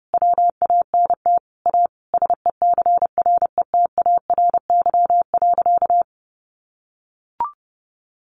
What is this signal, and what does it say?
Morse code, heard as a keyed tone, about 30 wpm, 700 Hz. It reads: WANT A SECRETARY.
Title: Sentences from Top 1300 Words 30wpm